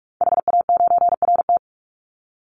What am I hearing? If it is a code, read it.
Morse code, heard as a keyed tone, about 44 wpm, 700 Hz. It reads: HA9RT
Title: Call Signs Repeated 2x - 45wpm